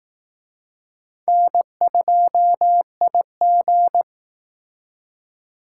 Morse code, heard as a keyed tone, about 19 wpm, 700 Hz. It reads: N2IG
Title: Speed Racing - USA 1x2 Call Signs 15wpm